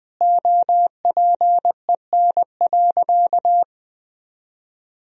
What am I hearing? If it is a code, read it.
Morse code, heard as a keyed tone, about 20 wpm, 700 Hz. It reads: OPEN.
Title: Sentences from Top 1000 Words - Review Part 3 - 20wpm